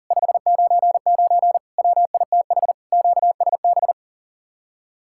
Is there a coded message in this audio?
Yes: Morse code from 0.1 to 3.9 s, heard as a keyed tone, about 52 wpm, 700 Hz. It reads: 599 WITH QSB